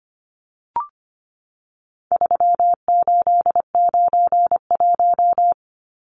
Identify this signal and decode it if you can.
Morse code, heard as a keyed tone, about 25 wpm, 700 Hz. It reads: 3891
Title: Numbers - 4-Digit 25wpm